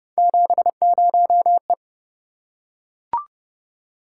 Morse code, heard as a keyed tone, about 26 wpm, 700 Hz. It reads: 70E